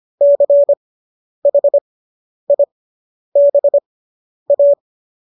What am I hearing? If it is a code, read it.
Morse code, heard as a keyed tone, about 25 wpm, spaced as for 10 wpm, 570 Hz. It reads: CHIBA
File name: qso005 10